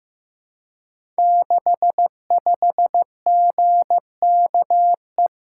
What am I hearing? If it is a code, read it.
Morse code, heard as a keyed tone, about 15 wpm, 700 Hz. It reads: 65GKE
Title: Code Groups - 5 Letter:Numbers - 15wpm